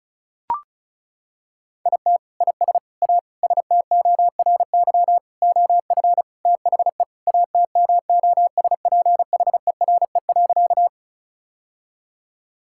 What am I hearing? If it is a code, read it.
Morse code, heard as a keyed tone, about 34 wpm, 700 Hz. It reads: IT IS A STORY OF THE ATMOSPHERE.
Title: Sentences from Top 1600 Words 35wpm